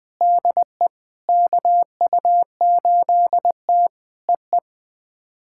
Morse code, heard as a keyed tone, about 20 wpm, 700 Hz. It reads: DE KU8T EE
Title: POTA QSO Practice - 20wpm